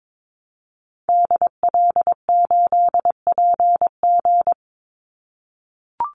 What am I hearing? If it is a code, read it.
Morse code, heard as a keyed tone, about 22 wpm, 700 Hz. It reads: DL8PG
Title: Call Signs - 22wpm